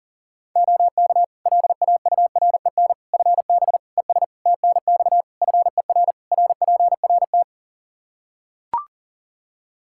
Morse code, heard as a keyed tone, about 40 wpm, 700 Hz. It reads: OK LAUREN FB ES TNX FER RPRT
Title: Practice QSO - Exchange 2 - Info - 40wpm